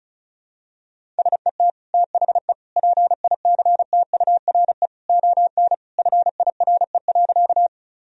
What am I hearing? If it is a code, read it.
Morse code, heard as a keyed tone, about 36 wpm, 700 Hz. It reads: SET THE PICTURE ON FIRE.
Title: Sentences from Top 1000 Words - Review Part 1 - 35wpm